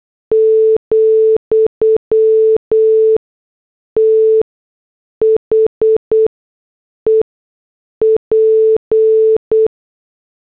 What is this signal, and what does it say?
Morse code, heard as a keyed tone, about 8 wpm, spaced as for 6 wpm, 440 Hz. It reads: , THE P